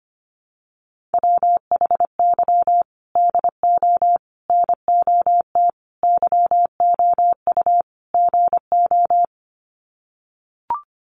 Morse code, heard as a keyed tone, about 25 wpm, 700 Hz. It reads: WHY DO NOT YOU GO
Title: Sets of 5 Words Encore 25wpm